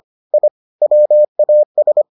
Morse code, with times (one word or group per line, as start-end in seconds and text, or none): 0.3-0.5 I
0.8-2.0 WAS